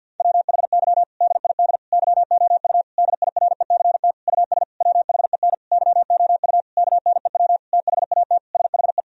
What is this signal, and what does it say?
Morse code, heard as a keyed tone, about 50 wpm, 700 Hz. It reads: WHY DID YOU DIRECT US WHEN YOU KNEW THAT SHE
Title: Sentences from Top 1000 Words - Review Part 2 - 50wpm